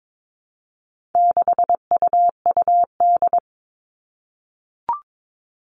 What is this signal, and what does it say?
Morse code, heard as a keyed tone, about 22 wpm, 700 Hz. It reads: 6UUD
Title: All Cut Numbers - 4-Digit 22wpm